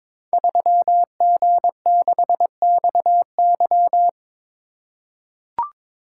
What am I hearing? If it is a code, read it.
Morse code, heard as a keyed tone, about 22 wpm, 690 Hz. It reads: 3G6XY
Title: Code Groups - 5 Letter:Numbers - 22wpm